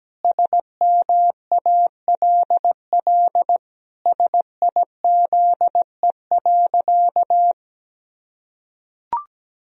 Morse code, heard as a keyed tone, about 17 wpm, 700 Hz. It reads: SMALL SIZE.
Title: Sentences from Top 1000 Words - Review Part 2 - 17wpm